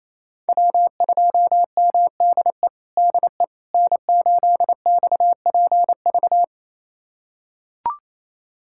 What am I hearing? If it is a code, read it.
Morse code, heard as a keyed tone, about 28 wpm, 700 Hz. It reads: W2MDE DE N8XPV
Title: USA Callsign DE Callsign - 5 Characters - 28wpm